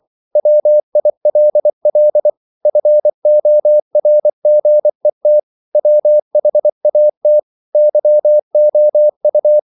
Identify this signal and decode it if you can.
Morse code, heard as a keyed tone, about 24 wpm, 600 Hz. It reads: WILL FORGET WHAT YOU
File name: CW / Practice / EnglishQuotations0224